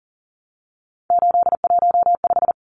Morse code, heard as a keyed tone, about 40 wpm, 700 Hz. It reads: 815
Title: Numbers - 3-Digit 40wpm